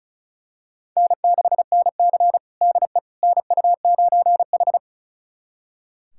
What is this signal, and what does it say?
Morse code, heard as a keyed tone, about 36 wpm, 700 Hz. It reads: N6NC DE NU9H
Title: USA Callsign DE Callsign - 4 Characters - 35wpm